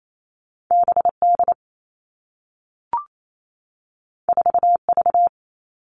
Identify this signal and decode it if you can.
Morse code, heard as a keyed tone, about 28 wpm, 700 Hz. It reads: BD 4V
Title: All Cut Numbers - 2-Digit 28wpm